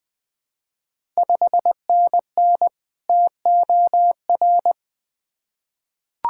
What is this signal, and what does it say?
Morse code, heard as a keyed tone, about 20 wpm, 700 Hz. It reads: 5NN TOR
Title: Ontario QSO Party Exchange 20wpm